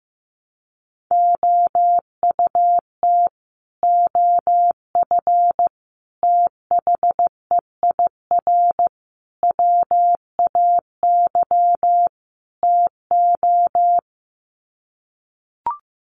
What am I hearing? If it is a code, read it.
Morse code, heard as a keyed tone, about 15 wpm, 700 Hz. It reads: OUT OF THEIR WAY TO
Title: Sets of 5 Words Encore2 15wpm